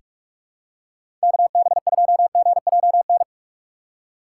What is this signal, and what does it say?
Morse code, heard as a keyed tone, about 42 wpm, 700 Hz. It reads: KB2GJN
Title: USA Call Signs - 6 Characters - 4x Character Spacing - 45wpm